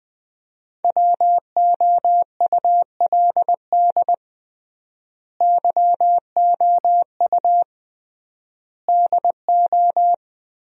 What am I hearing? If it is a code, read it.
Morse code, heard as a keyed tone, about 21 wpm, 700 Hz. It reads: WOULD YOU DO
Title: 3x Word-Spacing - Sentences from Top 100 Words - 20wpm